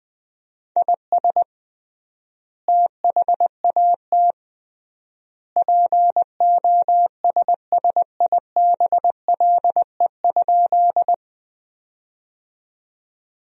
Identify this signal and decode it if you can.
Morse code, heard as a keyed tone, about 19 wpm, 700 Hz. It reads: IS THAT POSSIBLE?
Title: 3x Word-Spacing - Sentences from Top 500 Words - 20wpm